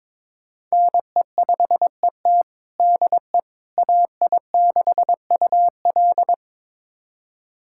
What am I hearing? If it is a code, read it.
Morse code, heard as a keyed tone, about 22 wpm, 700 Hz. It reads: NE5ET DE AI6UL
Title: USA Callsign DE Callsign - 5 Characters - 22wpm